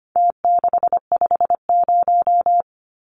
Morse code, heard as a keyed tone, about 25 wpm, 700 Hz. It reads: T650
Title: Numbers - Contest Style - 4-Digit 25wpm